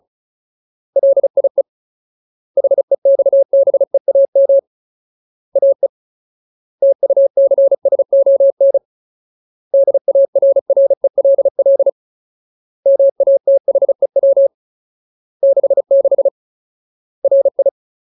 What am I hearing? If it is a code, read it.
Morse code, heard as a keyed tone, about 34 wpm, 550 Hz. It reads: LIE HEXBEAM AE TUCSON DARRELL MATHEW 66 RI
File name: Ultimate Word List / 35WPM Ultimate